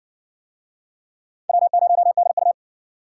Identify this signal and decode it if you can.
Morse code, heard as a keyed tone, about 60 wpm, 700 Hz. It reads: W0DU